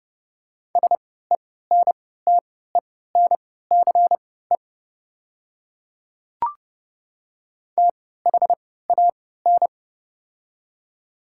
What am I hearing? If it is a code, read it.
Morse code, heard as a keyed tone, about 30 wpm, spaced as for 17 wpm, 700 Hz. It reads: SENTENCE THAN
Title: 3x Character Spacing - Top 300 Words - 30wpm